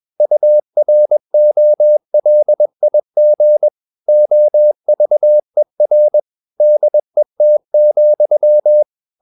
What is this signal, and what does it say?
Morse code, heard as a keyed tone, about 21 wpm, 600 Hz. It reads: UROLIG OVER DET,
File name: Ferie QRS 21wpm